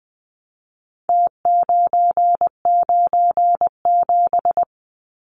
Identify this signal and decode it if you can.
Morse code, heard as a keyed tone, about 20 wpm, 700 Hz. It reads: T997
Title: Numbers - Contest Style - 4-Digit 20wpm